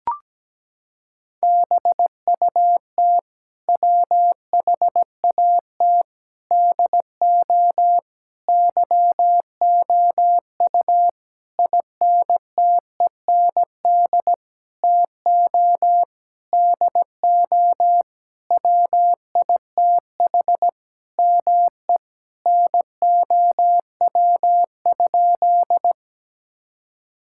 Morse code, heard as a keyed tone, about 17 wpm, 700 Hz. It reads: BUT WHAT DO YOU INTEND TO DO WITH ME NOW?
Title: Sentences from Top 1600 Words 17wpm